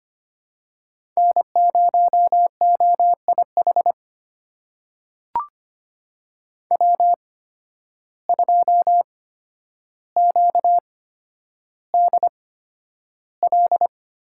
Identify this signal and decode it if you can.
Morse code, heard as a keyed tone, about 25 wpm, spaced as for 11 wpm, 700 Hz. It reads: N0OIH W2QDL
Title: USA Call Signs - 5 Characters - 8x Character Spacing - 25wpm